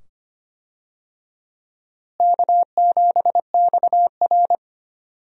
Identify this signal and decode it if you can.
Morse code, heard as a keyed tone, about 25 wpm, 700 Hz. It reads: K7XR
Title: USA Call Signs - 4 Characters - Repeated 2x - 25wpm